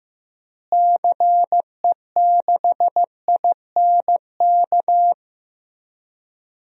Morse code, heard as a keyed tone, about 15 wpm, 700 Hz. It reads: CE6INK
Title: DXCC-100 Call Signs - 6 Characters - Repeated 2x - 15wpm